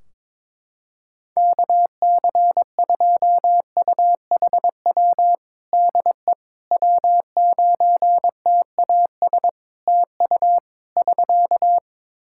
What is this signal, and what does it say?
Morse code, heard as a keyed tone, about 22 wpm, 700 Hz. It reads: KC2UHW DE W9TAS TU <SK>
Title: Practice QSO - Exchange 4 - Ending - 22wpm